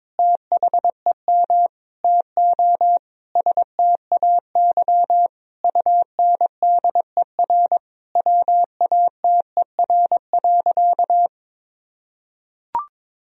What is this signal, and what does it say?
Morse code, heard as a keyed tone, about 22 wpm, 700 Hz. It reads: THEM TO STAY UNDER WATER.
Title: Sentences from Top 1000 Words - Review Part 2 - 22wpm